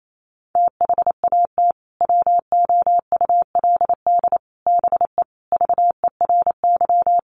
Morse code, heard as a keyed tone, about 28 wpm, 700 Hz. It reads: THAT WOULD BE VERY
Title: Sentences from Top 200 Words 28wpm